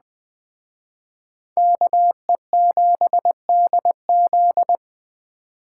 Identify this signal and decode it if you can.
Morse code, heard as a keyed tone, about 20 wpm, 700 Hz. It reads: KE7DZ
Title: Speed Racing - USA 2x2 Call Signs 17wpm